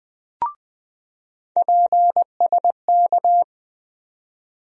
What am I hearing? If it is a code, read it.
Morse code, heard as a keyed tone, about 20 wpm, 700 Hz. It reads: PSK
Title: USA Call Sign Postfixes - 3 Letters 20wpm